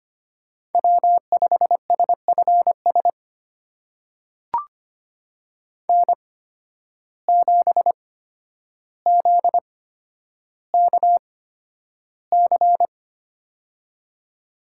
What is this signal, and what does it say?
Morse code, heard as a keyed tone, about 25 wpm, spaced as for 11 wpm, 700 Hz. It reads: W5SFS N7ZKC
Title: USA Call Signs - 5 Characters - 8x Character Spacing - 25wpm